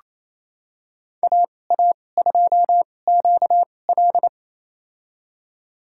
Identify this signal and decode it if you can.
Morse code, heard as a keyed tone, about 28 wpm, spaced as for 20 wpm, 700 Hz. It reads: AA2QL
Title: USA Call Signs - 5 Characters - 2x Character Spacing - 28wpm